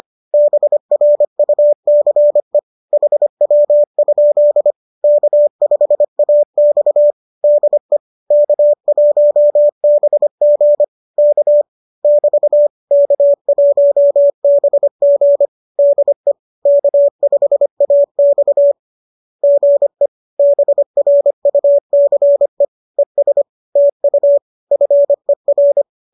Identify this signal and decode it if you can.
Morse code, heard as a keyed tone, about 25 wpm, 590 Hz. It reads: BRUCE HW? K5AX DE K1BG K <BT> K1BG DE K5AX GE BRUCE ES TU FER